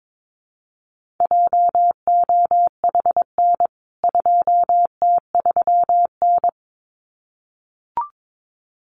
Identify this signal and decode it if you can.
Morse code, heard as a keyed tone, about 22 wpm, 700 Hz. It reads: JOHN 2T3N
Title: CWT Exchange 22wpm